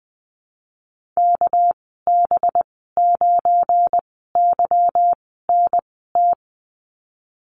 Morse code, heard as a keyed tone, about 20 wpm, spaced as for 15 wpm, 700 Hz. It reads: KB9YNT
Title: 200 Lesson 32g - Letter k - Call Signs 20wpm